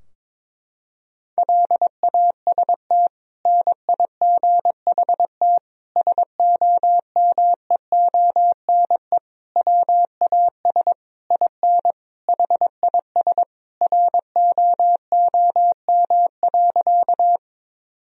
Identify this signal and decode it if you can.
Morse code, heard as a keyed tone, about 22 wpm, 700 Hz. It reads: LAST NIGHT SOMEONE WAS IN HIS ROOM.
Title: Sentences from Top 1000 Words - Review Part 4 - 22wpm